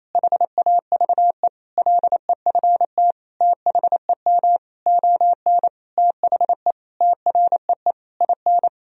Morse code, heard as a keyed tone, about 28 wpm, 700 Hz. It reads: HAVE LEFT THEM ON THE TREE IN
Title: Sentences from Top 1000 Words - Review Part 1 - 28wpm